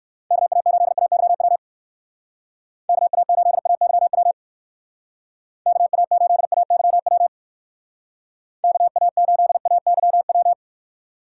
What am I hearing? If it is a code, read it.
Morse code, heard as a keyed tone, about 59 wpm, 700 Hz. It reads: KA8AYW KA8AYW KA8AYW KA8AYW